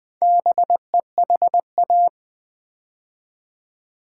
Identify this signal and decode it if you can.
Morse code, heard as a keyed tone, about 20 wpm, 700 Hz. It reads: BEHA
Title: Build A Word - 2 Letters At A Time - 6-Letter Words - 20wpm